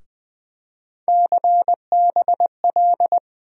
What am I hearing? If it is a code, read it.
Morse code, heard as a keyed tone, about 20 wpm, 700 Hz. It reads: CBL
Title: USA Call Sign Postfixes - 3 Letters 20wpm